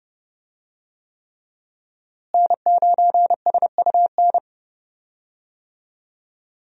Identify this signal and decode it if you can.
Morse code, heard as a keyed tone, about 30 wpm, 700 Hz. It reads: N9SUN
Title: USA Call Signs - 5 Characters - Repeated 3x - 30wpm